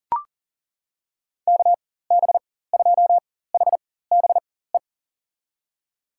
Morse code, heard as a keyed tone, about 40 wpm, spaced as for 19 wpm, 700 Hz. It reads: KB2HBE
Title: USA Call Signs - 4x Character Spacing - 40wpm